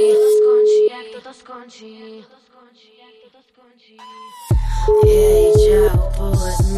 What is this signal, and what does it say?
Ringback tone (Europe, 425 Hz), with speech.